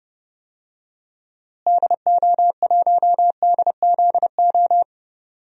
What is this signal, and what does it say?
Morse code, heard as a keyed tone, about 29 wpm, 700 Hz. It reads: DO1DZO
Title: DXCC-100 Call Signs - 6 Characters - Repeated 2x - 30wpm